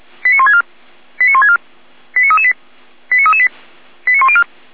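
Five-tone selective call (ZVEI), heard as quick runs of five tones, tones of about 70 ms. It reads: 89164 89164 89338 89338 89104
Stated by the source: ZVEI five-tone call received by SDR.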